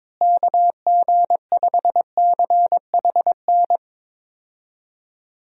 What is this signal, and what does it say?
Morse code, heard as a keyed tone, about 22 wpm, 700 Hz. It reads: KG5CHN
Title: USA Call Signs - 6 Characters - Repeated 3x - 22wpm